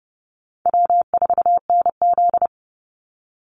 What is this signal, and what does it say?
Morse code, heard as a keyed tone, about 30 wpm, 700 Hz. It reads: W4NZ
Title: Easy Call Signs - 30wpm